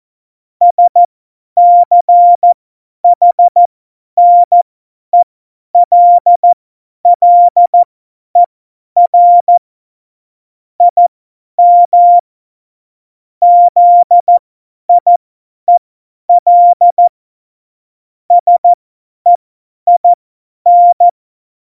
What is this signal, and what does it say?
Morse code, heard as a keyed tone, about 14 wpm, spaced as for 10 wpm, 700 Hz. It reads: SCHNELLER IM ZIEL SEIN